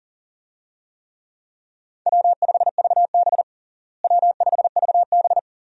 Morse code, heard as a keyed tone, about 40 wpm, 700 Hz. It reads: W5VB W5VB
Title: USA Call Signs - 4 Characters - Repeated 2x - 40wpm